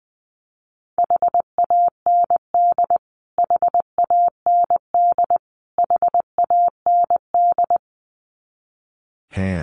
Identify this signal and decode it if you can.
Morse code, heard as a keyed tone, about 20 wpm, 700 Hz. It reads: HAND HAND HAND
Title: Top 200 Words Repeated 20wpm